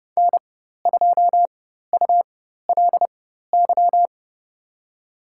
Morse code, heard as a keyed tone, about 31 wpm, spaced as for 14 wpm, 700 Hz. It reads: N2ULY
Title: USA Call Signs - 5 Characters - 4x Character Spacing - 30wpm